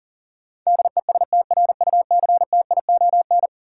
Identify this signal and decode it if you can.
Morse code, heard as a keyed tone, about 40 wpm, 700 Hz. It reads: DESTRUCTION
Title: Sets of 5 Words Encore - New Word Drill - 40wpm